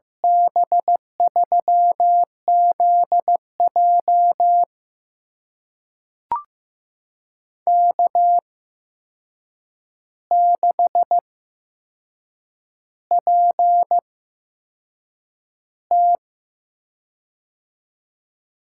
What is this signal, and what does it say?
Morse code, heard as a keyed tone, about 15 wpm, spaced as for 6 wpm, 700 Hz. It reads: B3ZJ K6PT
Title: USA Call Signs - 8x Character Spacing - 15wpm